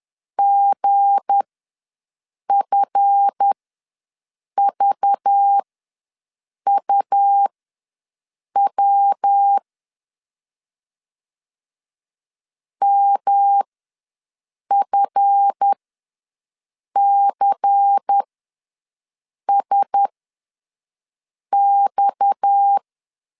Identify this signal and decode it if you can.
Morse code, heard as a keyed tone, about 10 wpm, spaced as for 5 wpm, 800 Hz. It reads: GFVUW MFCSX
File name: LEZIONI / 7 GF9Z / 7 GF ws 10wpm 3918